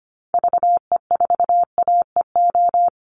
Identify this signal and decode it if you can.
Morse code, heard as a keyed tone, about 25 wpm, 700 Hz. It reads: VE4AEO
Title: Call Signs Repeated 2x - 25wpm